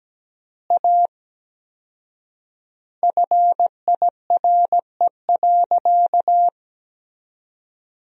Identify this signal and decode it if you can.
Morse code, heard as a keyed tone, about 17 wpm, 700 Hz. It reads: A FIRE.
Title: 4x Word-Spacing - Sentences from Top 400 Words - 17wpm